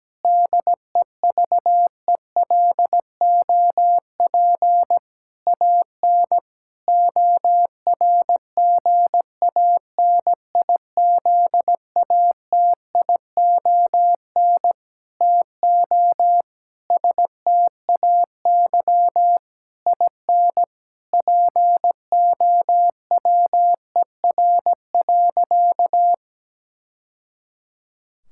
Morse code, heard as a keyed tone, about 17 wpm, 700 Hz. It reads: DEVELOP AN ORGANIZATION TO STAY IN POWER.
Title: Sentences from Top 1000 Words - Review Part 5 - 17wpm